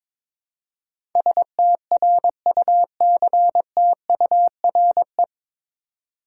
Morse code, heard as a keyed tone, about 22 wpm, 700 Hz. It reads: STRUCTURE
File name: 9 Letter Words - First 3 Letters Given - 22wpm